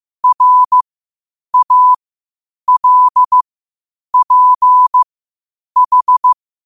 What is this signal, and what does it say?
Morse code, heard as a keyed tone, about 15 wpm, spaced as for 8 wpm, 1000 Hz. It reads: RALPH
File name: qotdfiles / 8wpm